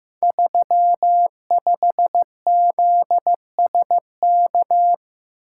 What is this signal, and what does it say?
Morse code, heard as a keyed tone, about 15 wpm, 700 Hz. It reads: 35ZSK
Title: Code Groups - 5 Letter:Numbers - 15wpm